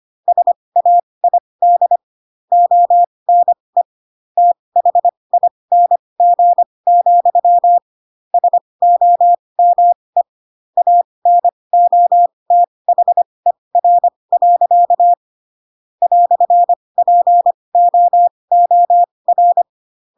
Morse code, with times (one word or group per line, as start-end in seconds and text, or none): 0.3-2.0 SAID
2.5-3.8 ONE
4.4-7.8 THING,
8.3-10.2 SOME
10.8-15.1 ANOTHER.
16.0-19.6 "POOR